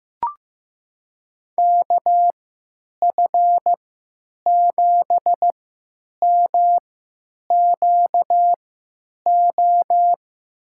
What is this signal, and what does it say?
Morse code, heard as a keyed tone, about 15 wpm, spaced as for 9 wpm, 700 Hz. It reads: KF7MQO
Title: USA Call Signs - 6 Characters - 3x Character Spacing - 15wpm